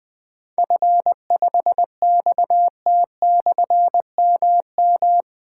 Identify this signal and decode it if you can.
Morse code, heard as a keyed tone, about 20 wpm, 700 Hz. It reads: F5XT/MM
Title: Difficult Call Signs 20wpm